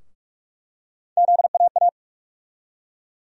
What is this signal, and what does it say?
Morse code, heard as a keyed tone, about 46 wpm, 700 Hz. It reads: ZAA
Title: USA Call Sign Postfixes - 3 Letters 45wpm